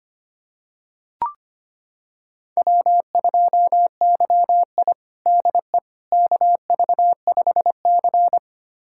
Morse code, heard as a keyed tone, about 25 wpm, 700 Hz. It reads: W2YI DE KV5C
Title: USA Callsign DE Callsign - 4 Characters - 25wpm